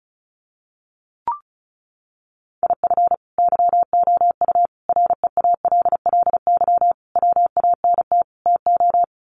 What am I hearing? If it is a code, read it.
Morse code, heard as a keyed tone, about 34 wpm, 700 Hz. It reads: IF YOU REALLY WANT TO
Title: Sets of 5 Words Encore2 35wpm